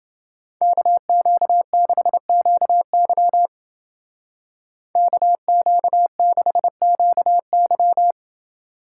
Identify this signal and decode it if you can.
Morse code, heard as a keyed tone, about 28 wpm, 700 Hz. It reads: KQ6QY KQ6QY